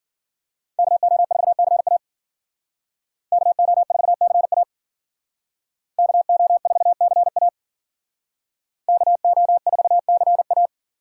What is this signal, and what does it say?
Morse code, heard as a keyed tone, about 52 wpm, 700 Hz. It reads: KO4CA KO4CA KO4CA KO4CA